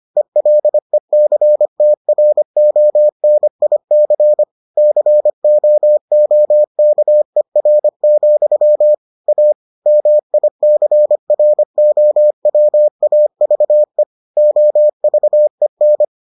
Morse code, heard as a keyed tone, about 25 wpm, 600 Hz. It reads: ELECTRONIC COOKER, A MICROWAVE OVEN